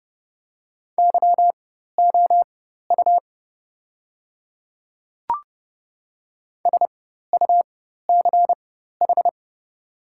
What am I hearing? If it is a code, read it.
Morse code, heard as a keyed tone, about 30 wpm, spaced as for 14 wpm, 700 Hz. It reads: YOU SUCH
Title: 4x Character Spacing - Top 300 Words - 30wpm